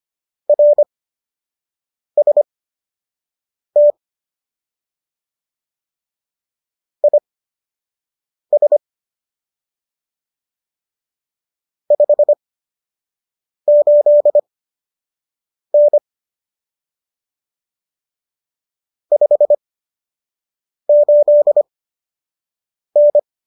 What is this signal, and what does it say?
Morse code, heard as a keyed tone, about 25 wpm, spaced as for 6 wpm, 590 Hz. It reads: RST IS 58N 58N